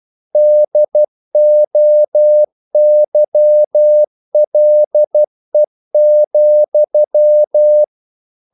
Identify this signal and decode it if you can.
Morse code, heard as a keyed tone, about 12 wpm, 600 Hz. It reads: DOYLE,
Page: English quotations in CW (telegraphy - morse)